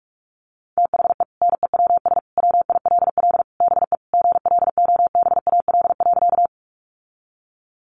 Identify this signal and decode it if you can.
Morse code, heard as a keyed tone, about 44 wpm, 700 Hz. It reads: THE NEWS WILL BE GLOBAL.
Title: Sentences from Top 1200 Words 45wpm